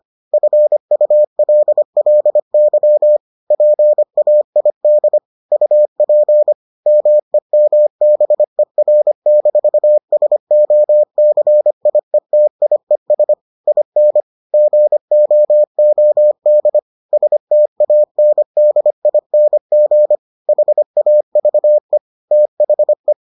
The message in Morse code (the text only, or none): FULLY PAID UP MEMBER-SOCIETIES IN GOOD STANDING HAVE THE